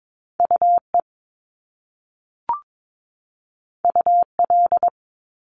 Morse code, heard as a keyed tone, about 22 wpm, 700 Hz. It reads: UE UL
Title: Top Two Letter Combinations in Words 22wpm